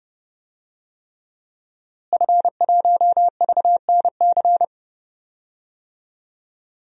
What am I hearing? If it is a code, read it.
Morse code, heard as a keyed tone, about 30 wpm, 700 Hz. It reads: F1VNC
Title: DXCC-100 Call Signs - 5 Characters - Repeated 3x - 30wpm